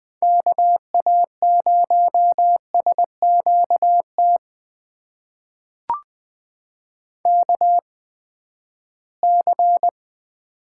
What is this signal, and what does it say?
Morse code, heard as a keyed tone, about 20 wpm, spaced as for 14 wpm, 700 Hz. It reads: KA0SQT KC
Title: USA Call Signs - 6 Characters - 8x Character Spacing - 20wpm